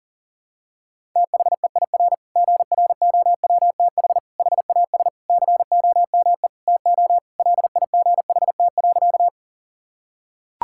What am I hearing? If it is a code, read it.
Morse code, heard as a keyed tone, about 40 wpm, 700 Hz. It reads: THEIR GROWTH HAS COME TO LIGHT.
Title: Sentences from Top 1000 Words - Review Part 5 - 40wpm